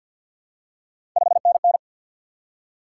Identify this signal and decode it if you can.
Morse code, heard as a keyed tone, about 53 wpm, 700 Hz. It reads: 5NN
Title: Top 180 QSO Elements 50wpm